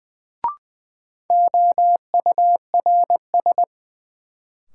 Morse code, heard as a keyed tone, about 20 wpm, 700 Hz. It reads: OURS
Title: Sentences from Top 1300 Words - New Word Drill Alt - 20wpm